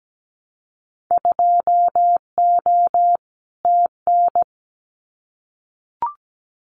Morse code, heard as a keyed tone, about 17 wpm, 700 Hz. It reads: 2O TN
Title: Winter Field Day Exchanges 17wpm